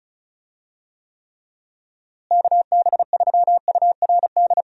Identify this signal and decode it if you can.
Morse code, heard as a keyed tone, about 34 wpm, 700 Hz. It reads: KB3URD
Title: USA Call Signs - 6 Characters - Repeated 3x - 35wpm